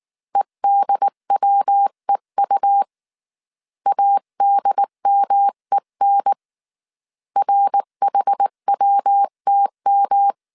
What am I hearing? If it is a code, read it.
Morse code, heard as a keyed tone, about 19 wpm, 800 Hz. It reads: EDWEU ADMEN RHWTM
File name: CW / DET / 1 3 1500C S B 18wpm 1916